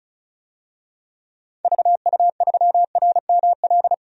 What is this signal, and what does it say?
Morse code, heard as a keyed tone, about 36 wpm, 700 Hz. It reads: VU3RML
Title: DXCC-100 Call Signs - 6 Characters - 35wpm